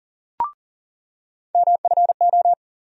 Morse code, heard as a keyed tone, about 40 wpm, 700 Hz. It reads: MFO
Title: USA Call Sign Postfixes - 3 Letters 40wpm